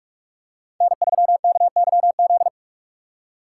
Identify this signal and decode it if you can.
Morse code, heard as a keyed tone, about 46 wpm, 700 Hz. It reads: N3KYZ
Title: USA Call Signs - 5 Characters - 4x Character Spacing - 45wpm